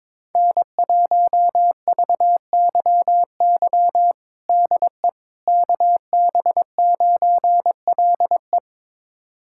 Morse code, heard as a keyed tone, about 21 wpm, 700 Hz. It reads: N1VYY DE KB9LE
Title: USA Callsign DE Callsign - 5 Characters - 22wpm